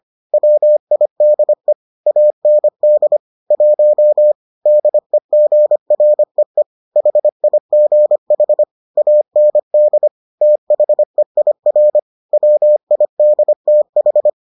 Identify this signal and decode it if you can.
Morse code, heard as a keyed tone, about 25 wpm, 600 Hz. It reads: WIDE AND 1 DEGREE HIGH AND THEIR WIDTH